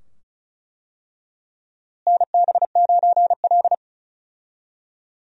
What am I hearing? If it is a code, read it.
Morse code, heard as a keyed tone, about 34 wpm, 700 Hz. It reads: NB9L
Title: USA Call Signs - 4 Characters - Repeated 2x - 35wpm